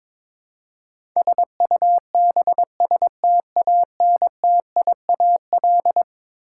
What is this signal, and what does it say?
Morse code, heard as a keyed tone, about 22 wpm, 700 Hz. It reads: SUBSTANTIAL
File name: practice / 22wpm / Eleven Letter Words - 22wpm